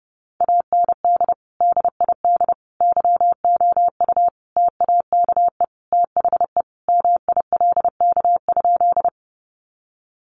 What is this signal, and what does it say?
Morse code, heard as a keyed tone, about 30 wpm, 700 Hz. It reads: AND DID YOU TAKE THE MILK?
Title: Sentences from Top 1500 Words 30wpm